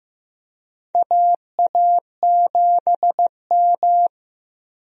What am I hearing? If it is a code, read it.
Morse code, heard as a keyed tone, about 15 wpm, 700 Hz. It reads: AA7M
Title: USA Call Signs - 4 Characters - Repeated 3x - 15wpm